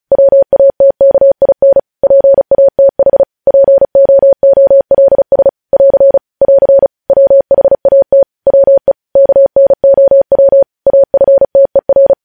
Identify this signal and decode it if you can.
Morse code, heard as a keyed tone, about 36 wpm, 550 Hz. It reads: WATKIN PATH POOLS <AR> <AR> WHAT WE KNOW AFTER
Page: news headlines in morse code at 35 wpm